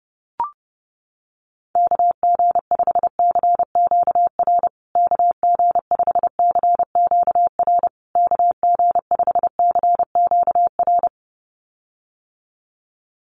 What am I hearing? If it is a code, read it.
Morse code, heard as a keyed tone, about 30 wpm, 700 Hz. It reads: KG5CQR KG5CQR KG5CQR
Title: USA Call Signs Repeated 3x 30wpm